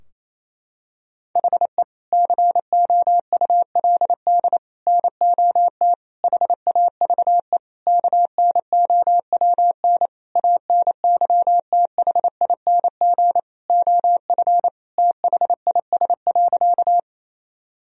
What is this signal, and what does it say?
Morse code, heard as a keyed tone, about 28 wpm, 700 Hz. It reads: HE COULD NOT HAVE KNOWN ANYTHING OF THIS.
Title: Sentences from Top 1000 Words - Review Part 3 - 28wpm